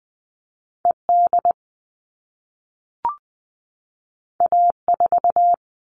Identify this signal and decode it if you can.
Morse code, heard as a keyed tone, about 20 wpm, 700 Hz. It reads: ED A4
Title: All Cut Numbers - 2-Digit 20wpm